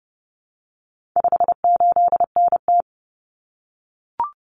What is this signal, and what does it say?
Morse code, heard as a keyed tone, about 30 wpm, 700 Hz. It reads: 58NT
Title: Common Cut Numbers - 4-Digit 30wpm